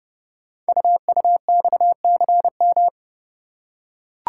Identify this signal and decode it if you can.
Morse code, heard as a keyed tone, about 30 wpm, 700 Hz. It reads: UUXCM
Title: Code Groups - 5 Letters - 30wpm